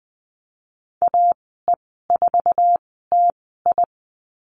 Morse code, heard as a keyed tone, about 19 wpm, spaced as for 14 wpm, 700 Hz. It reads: AE4TI
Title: 055 Lesson 10g - Letter r - Call Signs 20wpm